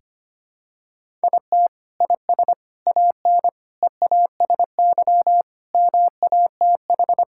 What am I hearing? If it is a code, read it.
Morse code, heard as a keyed tone, about 25 wpm, 700 Hz. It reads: IT IS AN EASY MATH
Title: Sentences from Top 1000 Words - Review Part 6 - 25wpm